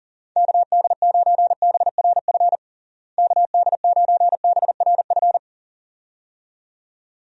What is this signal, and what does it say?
Morse code, heard as a keyed tone, about 40 wpm, 700 Hz. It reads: KD9BRF KD9BRF
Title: USA Call Signs - 6 Characters - Repeated 2x - 40wpm